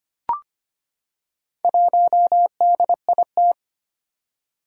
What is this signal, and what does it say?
Morse code, heard as a keyed tone, about 25 wpm, 700 Hz. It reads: 1DIT
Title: Four Challenging Letter-Number Combinations 25wpm